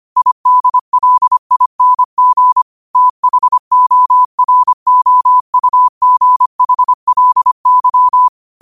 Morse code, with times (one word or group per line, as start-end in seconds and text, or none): 0.2-2.6 IDLING
3.0-8.3 THOROUGHLY